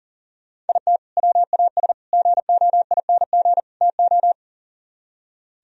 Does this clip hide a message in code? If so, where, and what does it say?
Yes: Morse code from 0.7 to 4.3 s, heard as a keyed tone, about 52 wpm, 700 Hz. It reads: IT WAS GOING TO